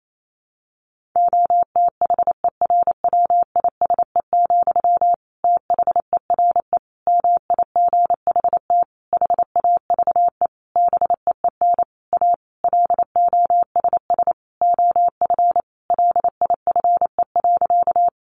Morse code, heard as a keyed tone, about 28 wpm, 700 Hz. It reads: OTHERWISE, THERE MIGHT HAVE BEEN A LOSS OF LIFE.
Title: Sentences from Top 1400 Words 28wpm